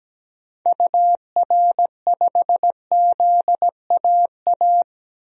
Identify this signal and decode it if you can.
Morse code, heard as a keyed tone, about 17 wpm, 700 Hz. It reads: UR5ZAA
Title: DXCC-100 Call Signs - 6 Characters - 17wpm